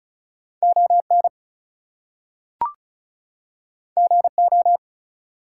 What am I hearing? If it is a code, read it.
Morse code, heard as a keyed tone, about 36 wpm, 700 Hz. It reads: ON GO
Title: Two Letter Words - 35wpm